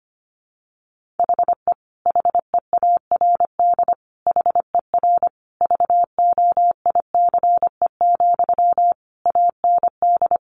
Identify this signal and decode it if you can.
Morse code, heard as a keyed tone, about 25 wpm, 700 Hz. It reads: HE HEARD HER VOICE, AND
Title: Sentences from Top 1100 Words 25wpm